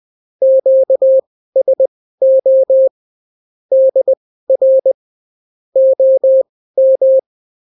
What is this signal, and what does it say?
Morse code, heard as a keyed tone, about 20 wpm, spaced as for 14 wpm, 530 Hz. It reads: QSO DR OM
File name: CW QSO libroCW4 20-10ppm